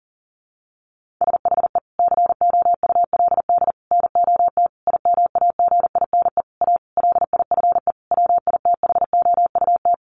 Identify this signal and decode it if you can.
Morse code, heard as a keyed tone, about 40 wpm, 700 Hz. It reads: SHE COULD NOT IMAGINE A LIFE WITHOUT
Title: Sentences from Top 1400 Words 40wpm